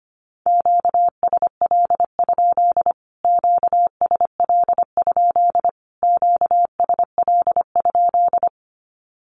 Morse code, heard as a keyed tone, about 25 wpm, 700 Hz. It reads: QSL? QSL? QSL?
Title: Q-Codes Repeated 3x 25wpm